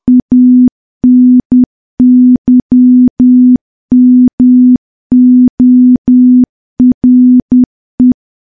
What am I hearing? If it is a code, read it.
Morse code, heard as a keyed tone, about 10 wpm, 250 Hz. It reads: ANYMORE
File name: morse / morse-training-10WPM